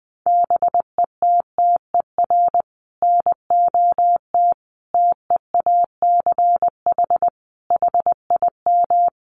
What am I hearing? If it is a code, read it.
Morse code, heard as a keyed tone, about 20 wpm, 700 Hz. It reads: BETTER NOT TEACH HIM
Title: Sentences from Top 600 Words 20wpm